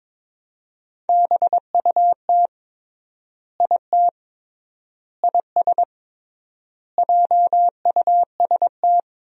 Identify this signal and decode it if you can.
Morse code, heard as a keyed tone, about 22 wpm, 700 Hz. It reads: BUT IT IS JUST
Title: 3x Word-Spacing - Sentences from Top 200 Words - 22wpm